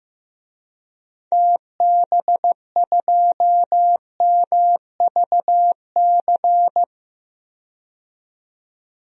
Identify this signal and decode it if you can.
Morse code, heard as a keyed tone, about 15 wpm, 700 Hz. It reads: TB2MVC
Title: DXCC-100 Call Signs - 6 Characters - 15wpm